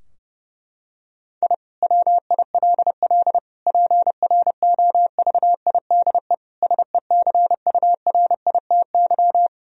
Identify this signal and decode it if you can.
Morse code, heard as a keyed tone, about 30 wpm, 700 Hz. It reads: I WILL PROVIDE SECURITY
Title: Sentences from Top 1000 Words - Review Part 4 - 30wpm